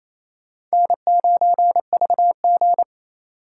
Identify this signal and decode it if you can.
Morse code, heard as a keyed tone, about 36 wpm, 700 Hz. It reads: N9VG